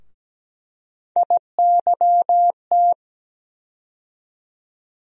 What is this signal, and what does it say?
Morse code, heard as a keyed tone, about 17 wpm, 700 Hz. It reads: IYT
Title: USA Call Sign Postfixes - 3 Letters 17wpm